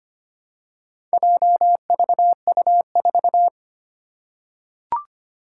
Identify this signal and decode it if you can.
Morse code, heard as a keyed tone, about 25 wpm, 700 Hz. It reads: JVU4
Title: Four Challenging Letter-Number Combinations 25wpm